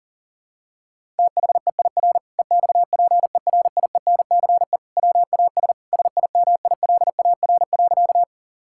Morse code, heard as a keyed tone, about 37 wpm, 700 Hz. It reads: THEIR EXPERIENCE WAS SIMILAR.
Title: Sentences from Top 1000 Words - Review Part 5 - 40wpm